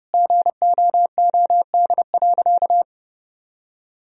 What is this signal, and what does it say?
Morse code, heard as a keyed tone, about 30 wpm, 700 Hz. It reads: GOOD.
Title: Sentences from Top 1000 Words - Review Part 3 - 30wpm